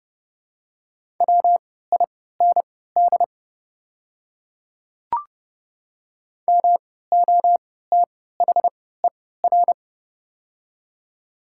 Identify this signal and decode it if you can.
Morse code, heard as a keyed tone, about 31 wpm, spaced as for 17 wpm, 700 Hz. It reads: WIND MOTHER
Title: 3x Character Spacing - Top 500 Words - 30wpm